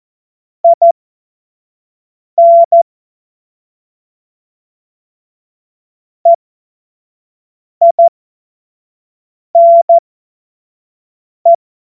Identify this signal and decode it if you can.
Morse code, heard as a keyed tone, about 14 wpm, spaced as for 5 wpm, 680 Hz. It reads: IN EINE
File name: Witz-05-August-2025-25-F70